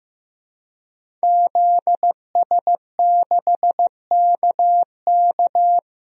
Morse code, heard as a keyed tone, about 15 wpm, 700 Hz. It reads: ZS6KK
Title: DXCC-100 Call Signs - 5 Characters - Repeated 3x - 15wpm